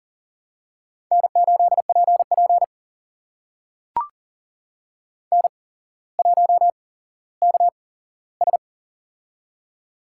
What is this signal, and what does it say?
Morse code, heard as a keyed tone, about 38 wpm, spaced as for 17 wpm, 700 Hz. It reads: N8PP N1KS
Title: USA Call Signs - 4 Characters - 8x Character Spacing - 40wpm